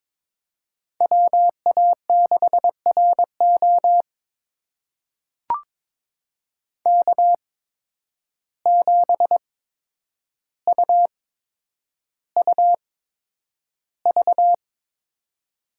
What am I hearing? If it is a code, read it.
Morse code, heard as a keyed tone, about 22 wpm, spaced as for 9 wpm, 700 Hz. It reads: WA6RO K7UUV
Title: USA Call Signs - 5 Characters - 8x Character Spacing - 22wpm